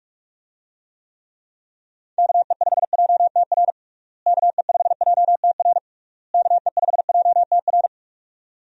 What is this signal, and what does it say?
Morse code, heard as a keyed tone, about 46 wpm, 700 Hz. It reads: KE5JTR KE5JTR KE5JTR
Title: USA Call Signs - 6 Characters - Repeated 3x - 45wpm